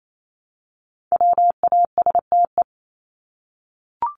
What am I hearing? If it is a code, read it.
Morse code, heard as a keyed tone, about 28 wpm, 700 Hz. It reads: WASTE
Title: Top 1751-2000 Words 28wpm